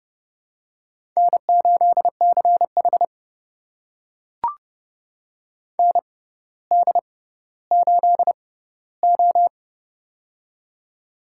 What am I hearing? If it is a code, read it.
Morse code, heard as a keyed tone, about 30 wpm, spaced as for 15 wpm, 700 Hz. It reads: N8CH ND8O
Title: USA Call Signs - 4 Characters - 6x Character Spacing - 30wpm